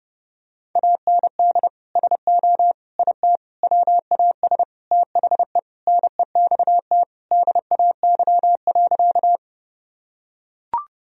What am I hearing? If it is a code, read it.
Morse code, heard as a keyed tone, about 30 wpm, 700 Hz. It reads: AND SO IT WAS THE NEXT DAY.
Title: Sentences from Top 1000 Words - Review Part 1 - 30wpm